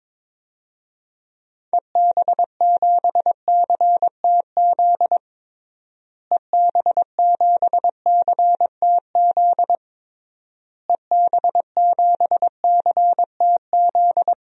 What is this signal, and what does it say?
Morse code, heard as a keyed tone, about 21 wpm, 700 Hz. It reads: EB7CTZ EB7CTZ EB7CTZ
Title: DXCC-100 Call Signs - 6 Characters - Repeated 3x - 22wpm